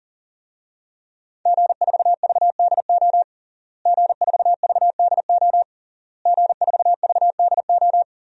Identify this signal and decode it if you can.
Morse code, heard as a keyed tone, about 40 wpm, 700 Hz. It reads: G4VDO G4VDO G4VDO
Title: DXCC-100 Call Signs - 5 Characters - Repeated 3x - 40wpm